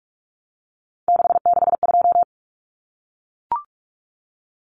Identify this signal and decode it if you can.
Morse code, heard as a keyed tone, about 44 wpm, 700 Hz. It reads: 662
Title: Numbers - 3-Digit 45wpm